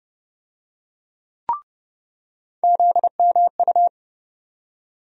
Morse code, heard as a keyed tone, about 30 wpm, 700 Hz. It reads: ZMU
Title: USA Call Sign Postfixes - 3 Letters 30wpm